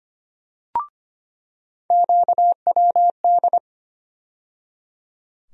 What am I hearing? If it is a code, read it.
Morse code, heard as a keyed tone, about 25 wpm, 700 Hz. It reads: QWD
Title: USA Call Sign Postfixes - 3 Letters 25wpm